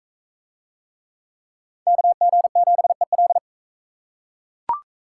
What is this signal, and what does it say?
Morse code, heard as a keyed tone, about 41 wpm, 700 Hz. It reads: KG7EL